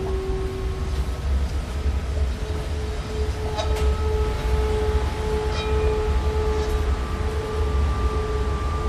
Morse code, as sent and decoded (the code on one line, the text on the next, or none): none